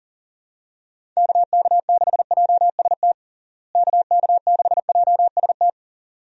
Morse code, heard as a keyed tone, about 40 wpm, 700 Hz. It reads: KK6JST KK6JST
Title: USA Call Signs - 6 Characters - Repeated 2x - 40wpm